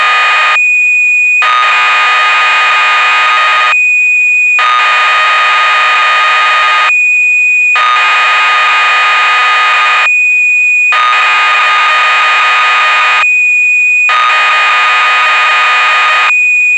Cassette data (Kansas City type).